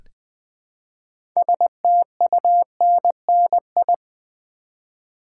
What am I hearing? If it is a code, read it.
Morse code, heard as a keyed tone, about 20 wpm, 700 Hz. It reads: STUNNI
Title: Build A Word - 2 Letters At A Time - 8-Letter Words - 20wpm